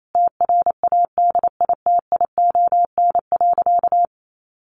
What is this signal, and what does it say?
Morse code, heard as a keyed tone, about 28 wpm, 700 Hz. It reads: TRADITION.
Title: Sentences from Top 1500 Words 28wpm